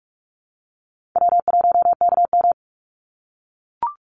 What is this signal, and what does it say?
Morse code, heard as a keyed tone, about 44 wpm, 700 Hz. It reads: W1KM
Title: Easy Call Signs - 45wpm